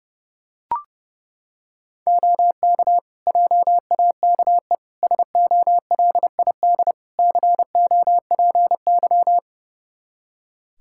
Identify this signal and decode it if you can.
Morse code, heard as a keyed tone, about 30 wpm, 700 Hz. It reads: OK JAKE SOLID COPY
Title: Practice QSO - Exchange 3 - More Info - 30wpm